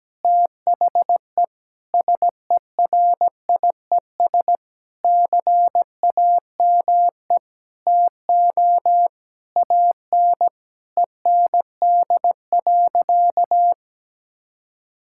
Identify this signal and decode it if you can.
Morse code, heard as a keyed tone, about 17 wpm, 700 Hz. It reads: THE SERIES CAME TO AN END.
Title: Sentences from Top 1000 Words - Review Part 5 - 17wpm